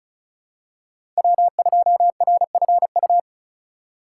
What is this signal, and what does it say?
Morse code, heard as a keyed tone, about 36 wpm, 700 Hz. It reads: W2RFU
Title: USA Call Signs - 5 Characters - 3x Character Spacing - 35wpm